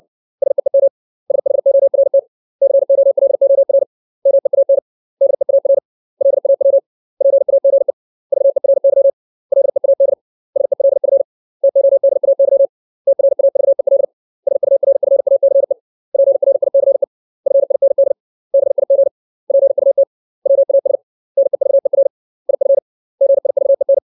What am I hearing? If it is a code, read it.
Morse code, heard as a keyed tone, about 60 wpm, 550 Hz. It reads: SEEM SHORT COLOR MEAN DEAR LAW PAGE VERY LEAD SELF TODAY TRAVEL SURFACE PRICE FIND BIG PUT WAS NEVER IF GIVEN
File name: 632 words / 632 Words 60WPM